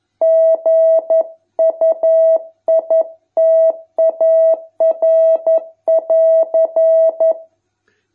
Morse code, heard as a keyed tone, about 11 wpm, 650 Hz. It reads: GUITAR<AR>